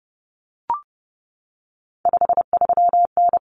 Morse code, heard as a keyed tone, about 30 wpm, 700 Hz. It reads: 53N
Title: Common Cut Numbers - 3-Digit 30wpm